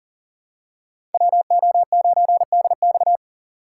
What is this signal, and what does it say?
Morse code, heard as a keyed tone, about 40 wpm, 700 Hz. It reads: WO9DX
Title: USA Call Signs - 5 Characters - 2x Character Spacing - 40wpm